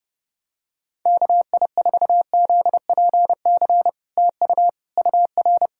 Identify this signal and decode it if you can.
Morse code, heard as a keyed tone, about 30 wpm, 700 Hz. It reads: KI4ZPC TU UR
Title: SOTA QSO Practice 30wpm